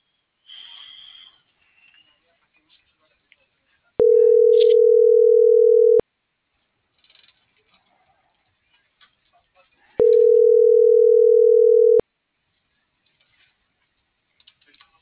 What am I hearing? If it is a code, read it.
Ringback tone (North America), 440 + 480 Hz.